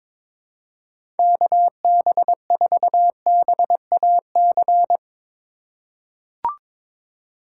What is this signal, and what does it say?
Morse code, heard as a keyed tone, about 22 wpm, 700 Hz. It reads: KB4BAC
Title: USA Call Signs - 6 Characters - 6x Character Spacing - 22wpm